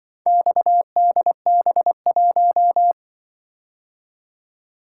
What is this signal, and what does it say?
Morse code, heard as a keyed tone, about 24 wpm, 700 Hz. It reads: XDB1
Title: Speed-Racing - Four Letter-Number Combinations - 20wpm